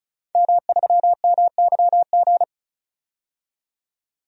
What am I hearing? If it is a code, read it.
Morse code, heard as a keyed tone, about 36 wpm, 700 Hz. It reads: M3MYG
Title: DXCC-100 Call Signs - 5 Characters - 35wpm